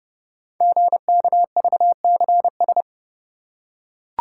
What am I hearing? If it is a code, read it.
Morse code, heard as a keyed tone, about 28 wpm, 700 Hz. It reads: GKVCS
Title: Code Groups - 5 Letters - 30wpm